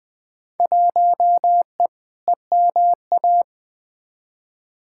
Morse code, heard as a keyed tone, about 20 wpm, 700 Hz. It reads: 1E EMA
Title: Field Day Exchanges Encore - 20wpm